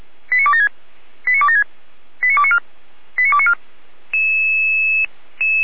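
Five-tone selective call (ZVEI), heard as quick runs of five tones, tones of about 70 ms. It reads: 89276 89276 89294 89294
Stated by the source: ZVEI five-tone selective call 89265, received by SDR.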